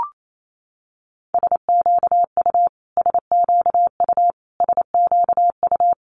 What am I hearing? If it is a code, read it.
Morse code, heard as a keyed tone, about 28 wpm, 700 Hz. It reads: SQU SQU SQU
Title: Multi-letter Phonemes Repeated 3x 28wpm